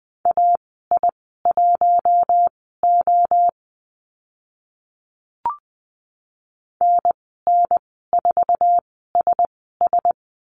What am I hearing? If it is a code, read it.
Morse code, heard as a keyed tone, about 20 wpm, spaced as for 14 wpm, 700 Hz. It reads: AI1O NN4SS
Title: 048 Lesson 9g - Number 4 - Call Signs 20wpm